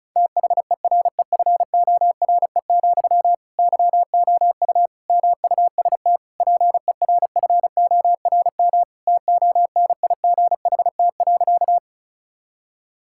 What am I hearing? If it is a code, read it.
Morse code, heard as a keyed tone, about 34 wpm, 690 Hz. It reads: THEREFORE, YOU MUST PERFORM TONIGHT.
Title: Sentences from Top 1000 Words - Review Part 5 - 35wpm